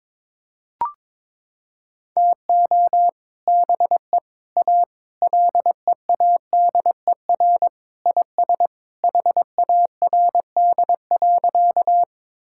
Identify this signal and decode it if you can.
Morse code, heard as a keyed tone, about 22 wpm, 700 Hz. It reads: TO BE A LEADER IS HARD.
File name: practice / 22wpm / Sentences with 3-Word Phrases - 22wpm